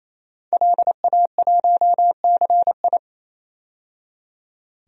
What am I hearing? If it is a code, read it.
Morse code, heard as a keyed tone, about 28 wpm, 700 Hz. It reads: LA1CI
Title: DXCC-100 Call Signs - 5 Characters - 28wpm